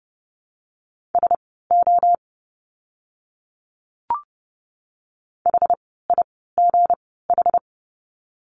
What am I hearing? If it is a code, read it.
Morse code, heard as a keyed tone, about 30 wpm, spaced as for 17 wpm, 700 Hz. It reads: SO HIGH
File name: course / 30:10wpm / 230 Lesson 37f - Letter x - Words 30wpm